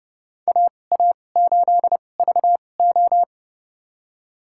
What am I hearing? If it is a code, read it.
Morse code, heard as a keyed tone, about 30 wpm, spaced as for 22 wpm, 700 Hz. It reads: AA8VO
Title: USA Call Signs - 5 Characters - 2x Character Spacing - 30wpm